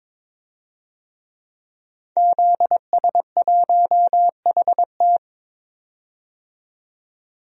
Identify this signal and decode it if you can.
Morse code, heard as a keyed tone, about 22 wpm, 700 Hz. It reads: ZS1HT
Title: DXCC-100 Call Signs - 5 Characters - 22wpm